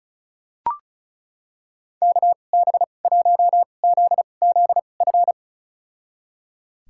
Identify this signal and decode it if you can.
Morse code, heard as a keyed tone, about 36 wpm, spaced as for 25 wpm, 700 Hz. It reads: KB1ZZF
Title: USA Call Signs - 6 Characters - 2x Character Spacing - 35wpm